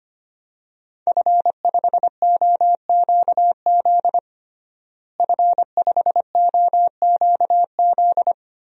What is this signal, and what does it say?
Morse code, heard as a keyed tone, about 25 wpm, 700 Hz. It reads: F5OQZ F5OQZ
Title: DXCC-100 Call Signs - 5 Characters - Repeated 2x - 25wpm